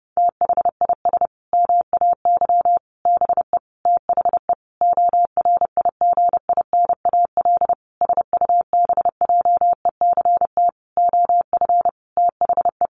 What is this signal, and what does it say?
Morse code, heard as a keyed tone, about 30 wpm, 700 Hz. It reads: THIS MAY BE THE ORIGINAL SUBJECT OF THE
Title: Sentences from Top 900 Words 30wpm